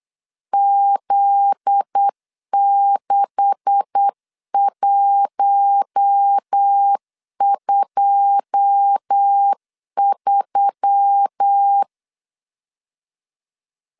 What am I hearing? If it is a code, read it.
Morse code, heard as a keyed tone, about 8 wpm, 800 Hz. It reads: Z6123
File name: LEZIONI / 9 286 8wpm 3425